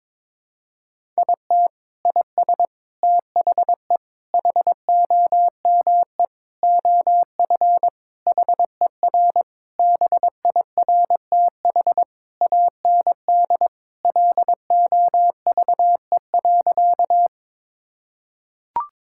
Morse code, heard as a keyed tone, about 22 wpm, 700 Hz. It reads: IT IS THE HOME OF HER BIRTH AND LOVE.
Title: Sentences from Top 1000 Words - Review Part 5 - 22wpm